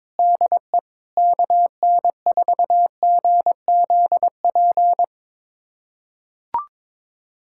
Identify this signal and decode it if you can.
Morse code, heard as a keyed tone, about 22 wpm, 700 Hz. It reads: DE KN4GZP
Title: USA Callsign DE Callsign - 6 Characters - 22wpm